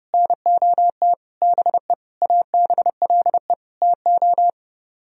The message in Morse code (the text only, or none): NOT BE ABLE TO